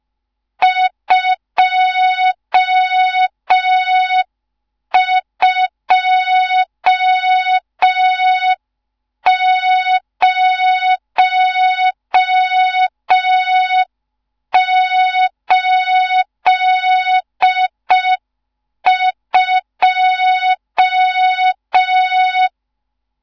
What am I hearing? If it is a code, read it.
Morse code, heard as a keyed tone, about 5 wpm, 3700 Hz, with noise. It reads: 22082
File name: Morze-HA2MN / hang / 14 280 HA2MN